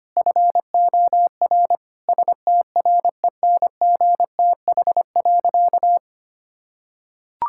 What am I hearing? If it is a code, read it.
Morse code, heard as a keyed tone, about 25 wpm, 700 Hz. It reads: FOR STRENGTH.
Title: Sentences from Top 1600 Words 25wpm